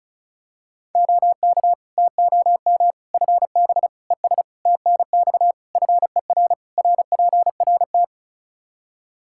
Morse code, heard as a keyed tone, about 34 wpm, 700 Hz. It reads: OK TOM FB ES TNX FER RPRT
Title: Practice QSO - Exchange 2 - Info - 35wpm